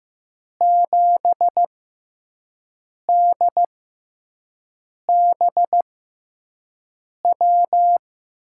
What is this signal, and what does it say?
Morse code, heard as a keyed tone, about 15 wpm, spaced as for 5 wpm, 700 Hz. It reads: 7DBW
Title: USA Call Signs - 6 Characters - 6x Character Spacing - 15wpm